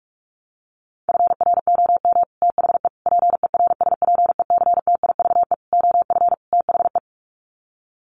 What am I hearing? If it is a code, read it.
Morse code, heard as a keyed tone, about 44 wpm, 700 Hz. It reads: FROM THE PERSPECTIVE OF THE
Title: Sets of 5 Words Encore2 45wpm